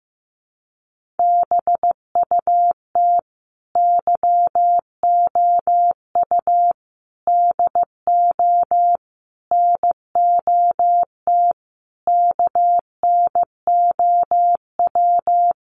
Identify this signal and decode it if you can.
Morse code, heard as a keyed tone, about 15 wpm, 700 Hz. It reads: BUT YOU DO NOT KNOW
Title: Sentences from Top 600 Words 15wpm